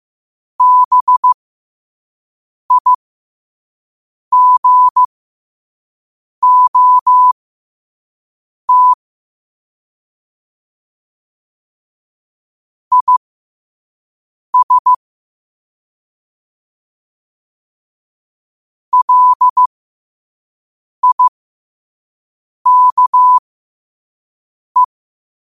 Morse code, heard as a keyed tone, about 15 wpm, spaced as for 5 wpm, 1000 Hz. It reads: BIGOT IS LIKE
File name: qotdfiles / 5wpm